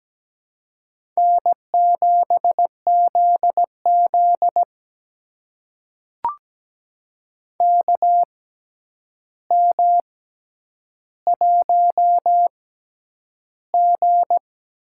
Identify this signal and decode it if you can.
Morse code, heard as a keyed tone, about 17 wpm, spaced as for 9 wpm, 700 Hz. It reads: N7ZZ KM1G
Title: USA Call Signs - 4 Characters - 6x Character Spacing - 17wpm